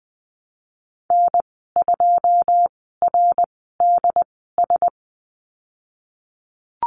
Morse code, heard as a keyed tone, about 20 wpm, spaced as for 15 wpm, 700 Hz. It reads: N2RDS
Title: 088 Lesson 15g - Number 5 - Call Signs 20wpm